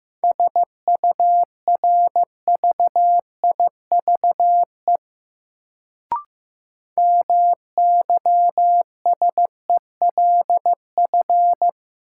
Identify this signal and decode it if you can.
Morse code, heard as a keyed tone, about 15 wpm, 700 Hz. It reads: SURVIVE MYSELF
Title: Sentences from Top 1300 Words - New Word Drill - 15wpm